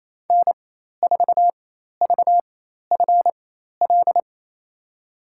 Morse code, heard as a keyed tone, about 28 wpm, spaced as for 13 wpm, 700 Hz. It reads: N4VFL
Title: USA Call Signs - 5 Characters - 4x Character Spacing - 28wpm